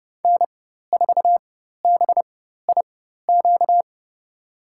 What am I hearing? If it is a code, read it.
Morse code, heard as a keyed tone, about 30 wpm, spaced as for 14 wpm, 700 Hz. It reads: N4BIQ
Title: USA Call Signs - 5 Characters - 4x Character Spacing - 30wpm